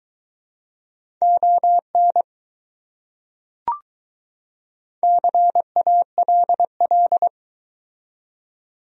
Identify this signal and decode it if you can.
Morse code, heard as a keyed tone, about 23 wpm, 700 Hz. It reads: ON CALL